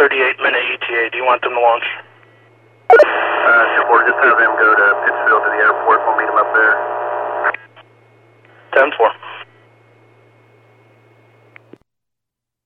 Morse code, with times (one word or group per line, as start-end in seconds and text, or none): none